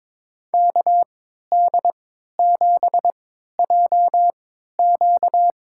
Morse code, heard as a keyed tone, about 22 wpm, spaced as for 13 wpm, 700 Hz. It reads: KD7JQ
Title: USA Call Signs - 5 Characters - 3x Character Spacing - 22wpm